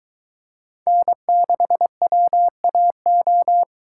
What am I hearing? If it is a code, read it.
Morse code, heard as a keyed tone, about 22 wpm, 700 Hz. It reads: N6WAO